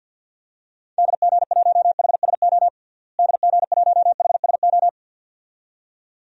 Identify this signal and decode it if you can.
Morse code, heard as a keyed tone, about 50 wpm, 700 Hz. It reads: DG1HSO DG1HSO
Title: DXCC-100 Call Signs - 6 Characters - Repeated 2x - 50wpm